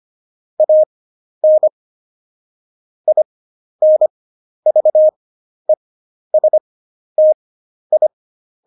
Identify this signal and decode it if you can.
Morse code, heard as a keyed tone, about 25 wpm, spaced as for 11 wpm, 620 Hz. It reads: AN INVESTI
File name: Jokes-25-char-10-eff